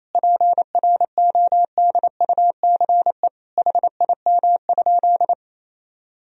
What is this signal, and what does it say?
Morse code, heard as a keyed tone, about 28 wpm, 700 Hz. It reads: PRODUCE HIM?
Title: Sentences from Top 1000 Words - Review Part 3 - 28wpm